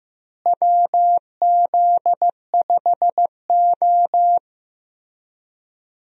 Morse code, heard as a keyed tone, about 15 wpm, 700 Hz. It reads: WZ5O
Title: USA Call Signs - 4 Characters - Repeated 3x - 15wpm